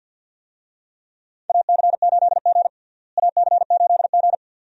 Morse code, heard as a keyed tone, about 48 wpm, 700 Hz. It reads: AC8G AC8G
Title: USA Call Signs - 4 Characters - Repeated 2x - 50wpm